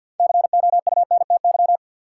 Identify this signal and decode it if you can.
Morse code, heard as a keyed tone, about 52 wpm, 700 Hz. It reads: COUNTY
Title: Sentences from Top 1200 Words - New Word Drill Alt - 50wpm